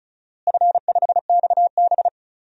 Morse code, heard as a keyed tone, about 36 wpm, 700 Hz. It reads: F5XB
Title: DXCC-100 Call Signs - 4 Characters - Repeated 3x - 35wpm